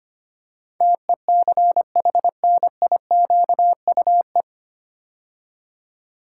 Morse code, heard as a keyed tone, about 25 wpm, 700 Hz. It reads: TECHNIQUE
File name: practice / 25wpm / 9 Letter Words - First 3 Letters Given - 25wpm